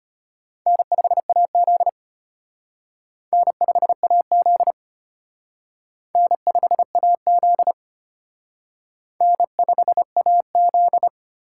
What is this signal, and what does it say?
Morse code, heard as a keyed tone, about 32 wpm, 700 Hz. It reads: N5AZ N5AZ N5AZ N5AZ